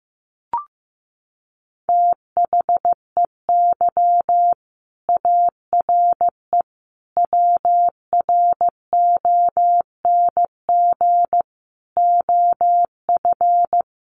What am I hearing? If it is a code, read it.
Morse code, heard as a keyed tone, about 15 wpm, 700 Hz. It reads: THEY ARE WRONG OF
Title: Sentences from Top 900 Words 15wpm